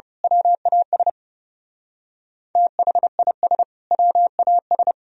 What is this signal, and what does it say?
Morse code, heard as a keyed tone, about 28 wpm, 700 Hz. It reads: WAS THIS WAS
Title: Speed-Racing - Sets of 2 Words 30wpm